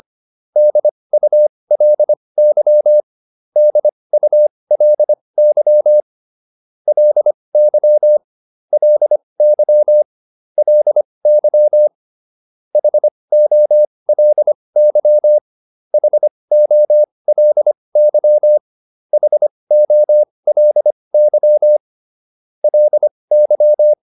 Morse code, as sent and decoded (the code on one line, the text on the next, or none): -.. ..- .-.. -.-- / -.. ..- .-.. -.-- / .-.. -.-- / .-.. -.-- / .-.. -.-- / .... --- .-.. -.-- / .... --- .-.. -.-- / .... --- .-.. -.-- / .-.. -.--
DULY DULY LY LY LY HOLY HOLY HOLY LY